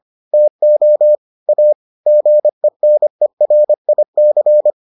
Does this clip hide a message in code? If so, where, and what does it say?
Yes: Morse code from 0.3 to 4.7 s, heard as a keyed tone, about 25 wpm, 600 Hz. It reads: TO A GENERIC